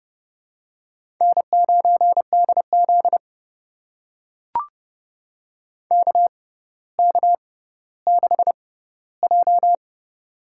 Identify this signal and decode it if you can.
Morse code, heard as a keyed tone, about 30 wpm, spaced as for 15 wpm, 700 Hz. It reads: N9DZ KK6J
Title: USA Call Signs - 6x Character Spacing - 30wpm